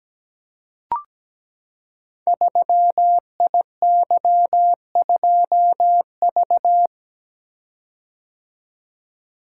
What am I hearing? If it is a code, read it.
Morse code, heard as a keyed tone, about 17 wpm, 700 Hz. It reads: 3IY2V
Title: Code Groups - 5 Letter:Numbers - 17wpm